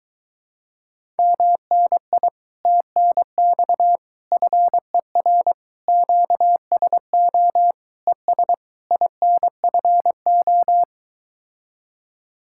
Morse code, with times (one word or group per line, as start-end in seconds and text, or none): 1.2-2.3 MNI
2.7-4.0 TNX
4.3-5.5 FER
5.9-7.7 QSO
8.1-8.5 ES
8.9-10.8 INFO